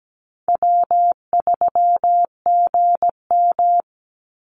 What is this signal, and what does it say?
Morse code, heard as a keyed tone, about 17 wpm, 700 Hz. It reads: W3GM
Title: USA Call Signs 17wpm